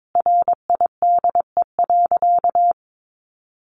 Morse code, heard as a keyed tone, about 22 wpm, 700 Hz. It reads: RIDE.
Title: Sentences from Top 1200 Words 22wpm